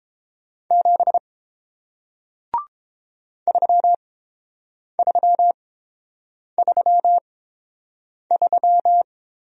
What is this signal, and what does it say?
Morse code, heard as a keyed tone, about 30 wpm, spaced as for 7 wpm, 700 Hz. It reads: 7 3333